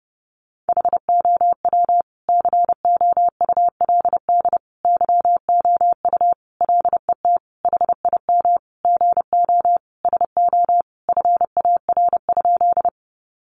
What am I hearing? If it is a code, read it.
Morse code, heard as a keyed tone, about 30 wpm, 700 Hz. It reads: HOW COULD YOU LET HIM GO SO FAR?
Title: Sentences from Top 300 Words 30wpm